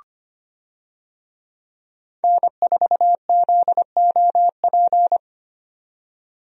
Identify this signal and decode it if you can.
Morse code, heard as a keyed tone, about 25 wpm, 700 Hz. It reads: N4ZOP
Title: USA Call Signs - 5 Characters - 25wpm